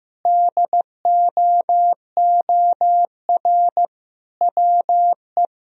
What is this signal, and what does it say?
Morse code, heard as a keyed tone, about 15 wpm, 700 Hz. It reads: DOOR WE
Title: Sentences from Top 1000 Words - Review Part 1 - 15wpm